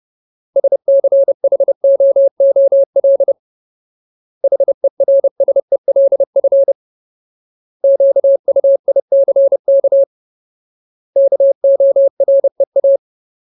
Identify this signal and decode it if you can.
Morse code, heard as a keyed tone, about 30 wpm, 550 Hz. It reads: SCHOOL HERSELF QUICK KOREA
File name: Ultimate Word List / 30WPM Ultimate